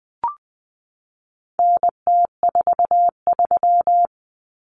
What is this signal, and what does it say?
Morse code, heard as a keyed tone, about 20 wpm, 700 Hz. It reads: NT43
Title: Common Cut Numbers - 4-Digit 20wpm